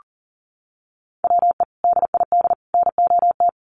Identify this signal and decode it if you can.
Morse code, heard as a keyed tone, about 40 wpm, 700 Hz. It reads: WE DID NOT
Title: Sets of 3 Words 40wpm